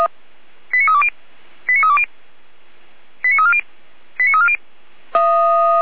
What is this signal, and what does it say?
Five-tone selective call (ZVEI), heard as quick runs of five tones, tones of about 70 ms. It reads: 89320 89320 89340 89340
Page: ZVEI